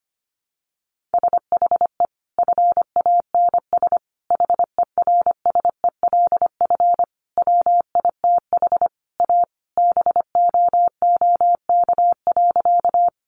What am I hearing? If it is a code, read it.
Morse code, heard as a keyed tone, about 25 wpm, 700 Hz. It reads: SHE FANS HERSELF WITH A BOOK.
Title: Additional Easy Phrases 25wpm